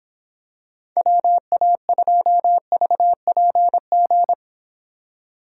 Morse code, heard as a keyed tone, about 24 wpm, 700 Hz. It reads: WA2VPG